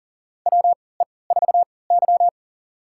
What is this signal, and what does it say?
Morse code, heard as a keyed tone, about 40 wpm, spaced as for 23 wpm, 700 Hz. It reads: WE4Y
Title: USA Call Signs - 4 Characters - 3x Character Spacing - 40wpm